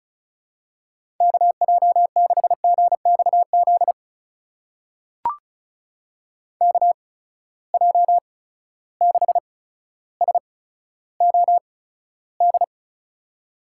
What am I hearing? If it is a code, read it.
Morse code, heard as a keyed tone, about 36 wpm, spaced as for 15 wpm, 700 Hz. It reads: KJ6GXZ KJ6SOD
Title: USA Call Signs - 6 Characters - 8x Character Spacing - 35wpm